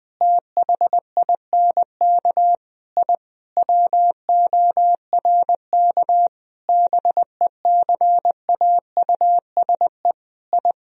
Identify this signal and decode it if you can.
Morse code, heard as a keyed tone, about 20 wpm, 700 Hz. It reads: THINK I WORK BECAUSE I
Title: Sentences from Top 1000 Words - Review Part 3 - 20wpm